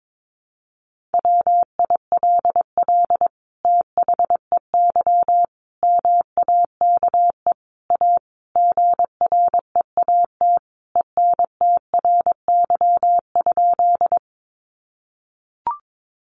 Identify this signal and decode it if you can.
Morse code, heard as a keyed tone, about 22 wpm, 700 Hz. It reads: WILL THEY MAKE A GREAT ENTRY?
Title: Sentences from Top 1500 Words 22wpm